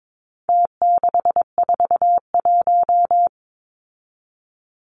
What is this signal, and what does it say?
Morse code, heard as a keyed tone, about 22 wpm, 700 Hz. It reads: T641
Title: Numbers - Contest Style - 4-Digit 22wpm